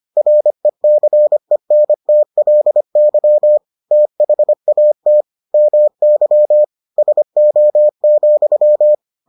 Morse code, heard as a keyed tone, about 25 wpm, 600 Hz. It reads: RECENTLY THAT MY SO,